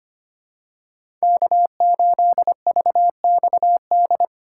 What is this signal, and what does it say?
Morse code, heard as a keyed tone, about 25 wpm, 700 Hz. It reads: K8VXD
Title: USA Call Signs - 5 Characters - 8x Character Spacing - 25wpm